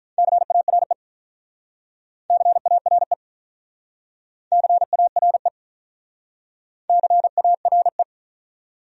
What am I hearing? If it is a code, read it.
Morse code, heard as a keyed tone, about 46 wpm, 700 Hz. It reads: CARE CARE CARE CARE